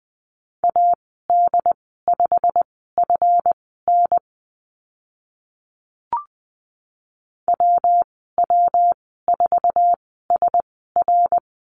Morse code, heard as a keyed tone, about 20 wpm, spaced as for 14 wpm, 700 Hz. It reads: AD5FN WW4SR
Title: 154 Lesson 25g - Letter p - Call Signs 20wpm